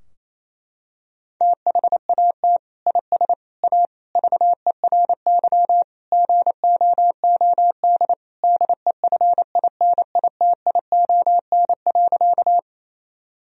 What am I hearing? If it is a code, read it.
Morse code, heard as a keyed tone, about 28 wpm, 700 Hz. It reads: THAT IS A VERY GOOD DEFINITION.
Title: Sentences from Top 1000 Words - Review Part 4 - 28wpm